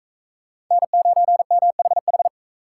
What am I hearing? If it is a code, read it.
Morse code, heard as a keyed tone, about 43 wpm, 700 Hz. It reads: N9MHH